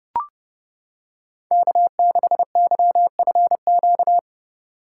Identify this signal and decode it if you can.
Morse code, heard as a keyed tone, about 30 wpm, 700 Hz. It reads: K6YFQ